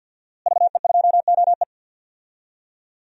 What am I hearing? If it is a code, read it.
Morse code, heard as a keyed tone, about 47 wpm, 700 Hz. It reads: VE2OE
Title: DXCC-100 Call Signs - 5 Characters - Repeated 2x - 50wpm